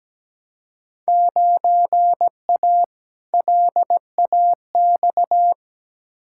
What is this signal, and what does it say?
Morse code, heard as a keyed tone, about 17 wpm, 700 Hz. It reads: 9A LAX
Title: Field Day Exchanges Encore - 17wpm